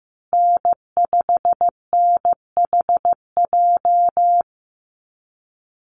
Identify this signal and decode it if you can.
Morse code, heard as a keyed tone, about 15 wpm, 700 Hz. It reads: N5NHJ
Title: Call Signs Repeated 2x - 15wpm